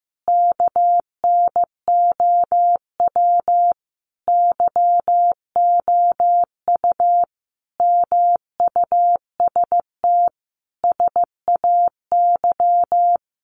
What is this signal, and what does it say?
Morse code, heard as a keyed tone, about 15 wpm, 700 Hz. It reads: KNOW YOU MUST SAY
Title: Sentences from Top 500 Words 15wpm